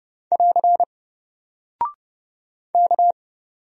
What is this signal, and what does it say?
Morse code, heard as a keyed tone, about 29 wpm, 700 Hz. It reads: <AR> K
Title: Single Character (Letter:Number:Punctuation:Prosign) Alt 30wpm